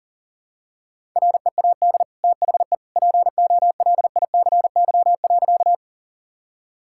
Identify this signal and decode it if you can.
Morse code, heard as a keyed tone, about 37 wpm, 700 Hz. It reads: READ THE POLICY.
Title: Sentences from Top 1000 Words - Review Part 5 - 40wpm